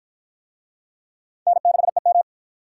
Morse code, heard as a keyed tone, about 50 wpm, 700 Hz. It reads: N6EM